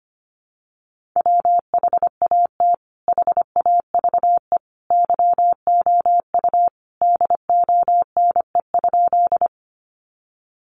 Morse code, heard as a keyed tone, about 25 wpm, 700 Hz. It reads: WHAT HAVE YOU DONE?
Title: Sentences from Top 500 Words 25wpm